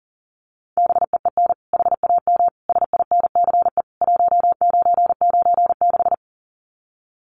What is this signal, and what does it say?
Morse code, heard as a keyed tone, about 40 wpm, 700 Hz. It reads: BEEN HAM SINCE 1996
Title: Common QSO Phrases 40wpm